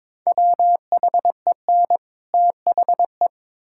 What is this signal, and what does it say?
Morse code, heard as a keyed tone, about 22 wpm, 700 Hz. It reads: WHEN THE
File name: Build A Phrase - 2-Word Phrases - 22wpm